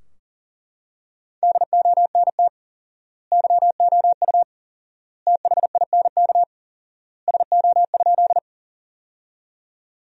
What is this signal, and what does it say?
Morse code, heard as a keyed tone, about 40 wpm, 700 Hz. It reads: DONT YOU THINK SO?
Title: 4x Word-Spacing - Sentences from Top 300 Words - 40wpm